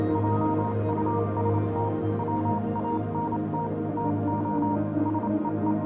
Morse code, heard as a keyed tone, about 27 wpm, 940 Hz, with noise. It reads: JUST WANT TO SEE
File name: Vylet Pony / aura / do you remember the song she sang (morse code version)